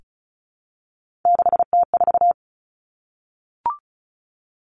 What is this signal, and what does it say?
Morse code, heard as a keyed tone, about 36 wpm, 700 Hz. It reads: 6T4
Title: Three Challenging Letter-Number Combinations 35wpm